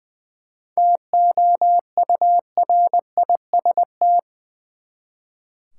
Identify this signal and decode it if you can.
Morse code, heard as a keyed tone, about 20 wpm, 700 Hz. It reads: TOURIST
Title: Seven Letter Words - 20wpm